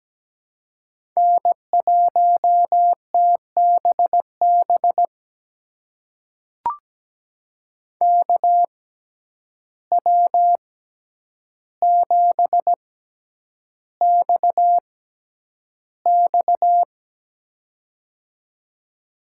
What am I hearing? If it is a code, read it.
Morse code, heard as a keyed tone, about 17 wpm, spaced as for 9 wpm, 700 Hz. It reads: N1TBB KW7XX
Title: USA Call Signs - 5 Characters - 6x Character Spacing - 17wpm